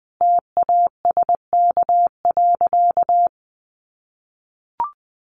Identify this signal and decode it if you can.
Morse code, heard as a keyed tone, about 20 wpm, 700 Hz. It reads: TASK.
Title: Sentences from Top 1300 Words 20wpm